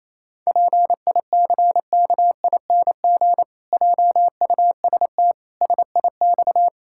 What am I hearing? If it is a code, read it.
Morse code, heard as a keyed tone, about 28 wpm, 700 Hz. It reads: PICKING JUST SIX
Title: Pangram Challenge v2 - 28wpm